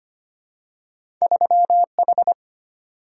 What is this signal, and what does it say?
Morse code, heard as a keyed tone, about 25 wpm, 700 Hz. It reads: 3H
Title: Single-Letter Single-Number 25wpm